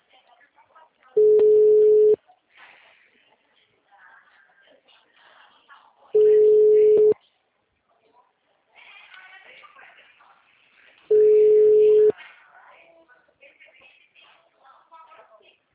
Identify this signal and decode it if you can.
Ringback tone (Europe, 425 Hz), with speech.
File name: out- -2014- - - .75194